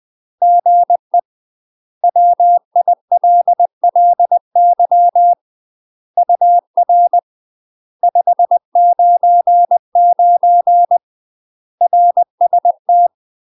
Morse code, heard as a keyed tone, about 20 wpm, 700 Hz. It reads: GE WILLY UR 599 RST
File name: UK MASTER QSOs TRAINING / 20WPM / UK MASTER QSOs 0049